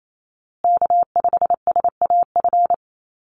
Morse code, heard as a keyed tone, about 28 wpm, 700 Hz. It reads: K5SAF
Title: USA Call Signs Repeated 2x 28wpm